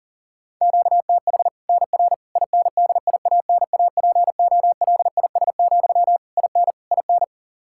Morse code, heard as a keyed tone, about 37 wpm, 700 Hz. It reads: QTH NR INDIANAPOLIS, IN IN
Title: Practice QSO - Exchange 1 - Beginning - 40wpm